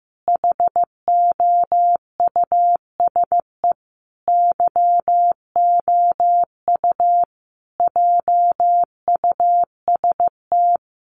Morse code, heard as a keyed tone, about 15 wpm, 700 Hz. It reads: HOUSE YOU JUST
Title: Sentences from Top 1200 Words 15wpm